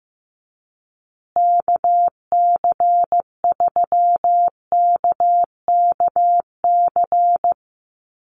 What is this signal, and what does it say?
Morse code, heard as a keyed tone, about 15 wpm, 700 Hz. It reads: KC3KKC
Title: USA Call Signs Repeated 3x 15wpm